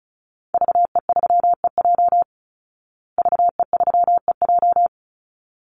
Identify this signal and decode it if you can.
Morse code, heard as a keyed tone, about 34 wpm, 700 Hz. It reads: VE3EJ VE3EJ
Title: Call Signs Repeated 3x - 35wpm